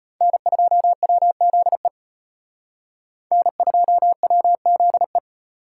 Morse code, heard as a keyed tone, about 31 wpm, 700 Hz. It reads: N2WZE N2WZE
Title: Speed Racing - USA 1x3 Call Signs 25wpm